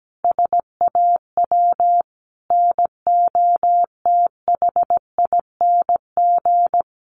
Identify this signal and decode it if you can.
Morse code, heard as a keyed tone, about 17 wpm, 700 Hz. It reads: SAW NOTHING
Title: Sentences from Top 600 Words 17wpm